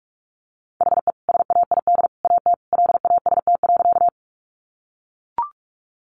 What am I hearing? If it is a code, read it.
Morse code, heard as a keyed tone, about 44 wpm, 700 Hz. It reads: HE SAID AT LAST.
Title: Sentences from Top 300 Words 45wpm